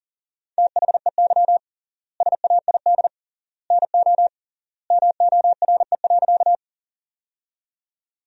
Morse code, heard as a keyed tone, about 40 wpm, 700 Hz. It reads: THEY SAID NO MORE.
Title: 3x Word-Spacing - Sentences from Top 100 Words - 40wpm